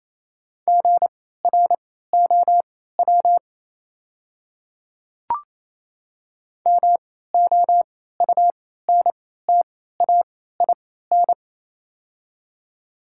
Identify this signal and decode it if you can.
Morse code, heard as a keyed tone, about 28 wpm, spaced as for 16 wpm, 700 Hz. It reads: GROW MOUNTAIN
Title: 3x Character Spacing - Top 300 Words - 28wpm